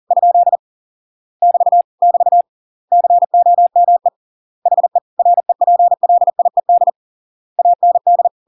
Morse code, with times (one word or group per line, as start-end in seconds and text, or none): 0.1-0.6 ?
1.4-1.8 <BT>
2.0-2.4 <BT>
2.9-4.1 COME
4.7-5.0 HE
5.2-6.9 REPLIED
7.6-8.3 AND